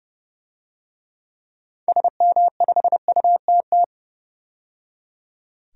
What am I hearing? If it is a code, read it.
Morse code, heard as a keyed tone, about 31 wpm, 700 Hz. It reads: SM5UTT
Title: DXCC-100 Call Signs - 6 Characters - 30wpm